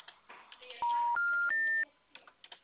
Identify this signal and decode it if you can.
Special information tone.
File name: out- -2004- - - .4